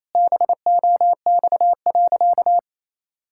Morse code, heard as a keyed tone, about 28 wpm, 700 Hz. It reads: BOX.
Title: Sentences from Top 1000 Words - Review Part 5 - 28wpm